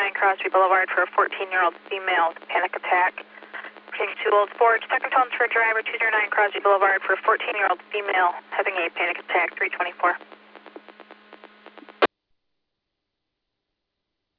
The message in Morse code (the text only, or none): none